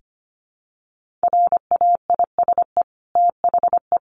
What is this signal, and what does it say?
Morse code, heard as a keyed tone, about 25 wpm, 700 Hz. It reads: RAISE THE
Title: Sentences from Top 1400 Words 25wpm